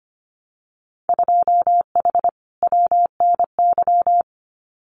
Morse code, heard as a keyed tone, about 25 wpm, 700 Hz. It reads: 2H WNY
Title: Winter Field Day Exchanges Encore 25wpm